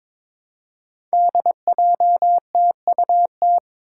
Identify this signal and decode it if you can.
Morse code, heard as a keyed tone, about 22 wpm, 700 Hz. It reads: DJTUT
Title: Five Challenging Letter-Number Combinations 22wpm